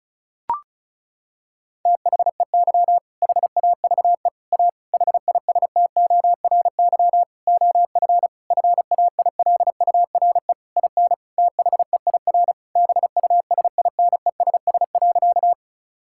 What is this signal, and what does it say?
Morse code, heard as a keyed tone, about 34 wpm, 700 Hz. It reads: THEY HAVE A HISTORY OF FAILURE IN THEIR BUSINESS.
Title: Sentences from Top 1000 Words - Review Part 6 - 35wpm